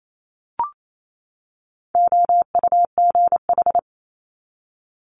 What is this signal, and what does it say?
Morse code, heard as a keyed tone, about 28 wpm, 700 Hz. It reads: OUGH
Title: Multi-letter Phonemes 28wpm